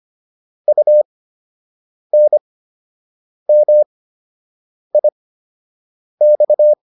Morse code, heard as a keyed tone, about 25 wpm, spaced as for 7 wpm, 600 Hz. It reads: UNMIX